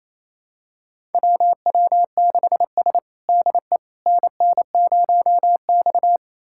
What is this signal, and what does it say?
Morse code, heard as a keyed tone, about 29 wpm, 700 Hz. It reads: WW6S DE NN0X
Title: USA Callsign DE Callsign - 4 Characters - 28wpm